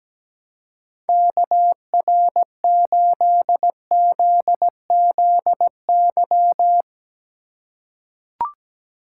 Morse code, heard as a keyed tone, about 17 wpm, 700 Hz. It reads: KR8ZZY
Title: USA Call Signs - 6 Characters - 3x Character Spacing - 17wpm